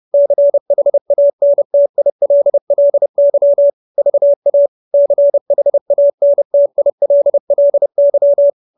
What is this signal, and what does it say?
Morse code, heard as a keyed tone, about 30 wpm, 570 Hz. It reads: CHANTILLY VA CHANTILLY
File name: QSO316 30